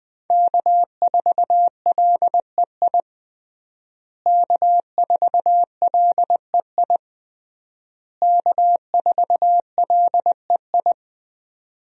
Morse code, heard as a keyed tone, about 20 wpm, 700 Hz. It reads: K4LEI K4LEI K4LEI
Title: USA Call Signs - 5 Characters - Repeated 3x - 20wpm